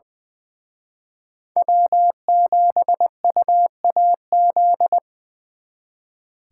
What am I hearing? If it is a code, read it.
Morse code, heard as a keyed tone, about 25 wpm, 700 Hz. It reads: W7UAZ
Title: Speed Racing - USA 1x3 Call Signs 20wpm